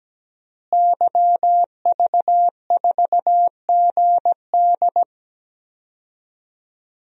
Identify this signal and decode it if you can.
Morse code, heard as a keyed tone, about 17 wpm, 700 Hz. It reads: YV4GD
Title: DXCC-100 Call Signs - 5 Characters - Repeated 2x - 17wpm